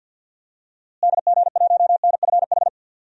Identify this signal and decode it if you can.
Morse code, heard as a keyed tone, about 52 wpm, 700 Hz. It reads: DG1NFH
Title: DXCC-100 Call Signs - 6 Characters - Repeated 3x - 50wpm